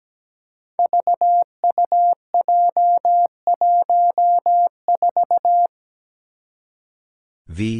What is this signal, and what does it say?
Morse code, heard as a keyed tone, about 17 wpm, 700 Hz. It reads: VUJ14
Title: Five Challenging Letter-Number Combinations 17wpm